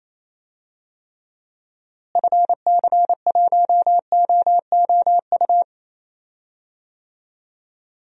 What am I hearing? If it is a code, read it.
Morse code, heard as a keyed tone, about 28 wpm, 700 Hz. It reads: FC1OOU
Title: DXCC-100 Call Signs - 6 Characters - Repeated 3x - 28wpm